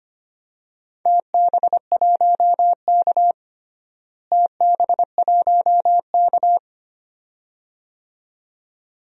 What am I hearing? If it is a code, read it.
Morse code, heard as a keyed tone, about 25 wpm, 700 Hz. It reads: TB1K TB1K
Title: DXCC-100 Call Signs - 4 Characters - Repeated 2x - 25wpm